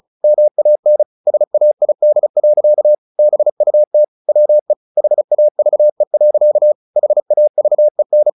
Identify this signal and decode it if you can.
Morse code, heard as a keyed tone, about 34 wpm, 600 Hz. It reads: MAN SAID. BUT WE HAVE. HAVEN
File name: the old man and the sea / the old man and the sea 1 35wpm